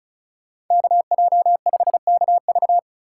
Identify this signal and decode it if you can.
Morse code, heard as a keyed tone, about 44 wpm, 700 Hz. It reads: KJ5KV
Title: Speed Racing - USA 2x2 Call Signs 35wpm